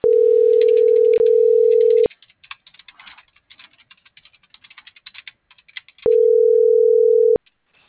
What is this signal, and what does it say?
Ringback tone (North America), 440 + 480 Hz.